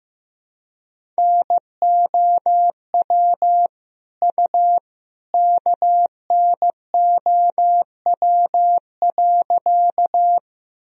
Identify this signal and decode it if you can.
Morse code, heard as a keyed tone, about 15 wpm, 700 Hz. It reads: NOW U KNOW.
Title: Sentences from Top 100 Words and Abbreviations 15wpm